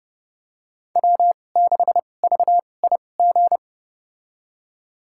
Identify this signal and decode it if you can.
Morse code, heard as a keyed tone, about 29 wpm, spaced as for 21 wpm, 700 Hz. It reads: W6VIG
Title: USA Call Signs - 5 Characters - 2x Character Spacing - 30wpm